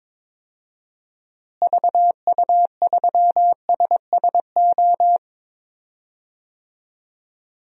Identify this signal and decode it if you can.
Morse code, heard as a keyed tone, about 22 wpm, 700 Hz. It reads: VU3SSO
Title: DXCC-100 Call Signs - 6 Characters - Repeated 2x - 22wpm